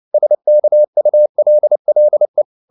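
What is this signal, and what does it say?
Morse code, heard as a keyed tone, about 29 wpm, 600 Hz. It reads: SKULLE
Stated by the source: Fast (QRQ) Morse at 29 wpm, from CW med Gnister.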